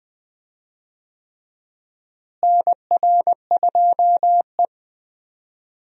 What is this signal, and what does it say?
Morse code, heard as a keyed tone, about 20 wpm, 700 Hz. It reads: NR2E
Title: USA Call Signs - 4 Characters - Repeated 3x - 20wpm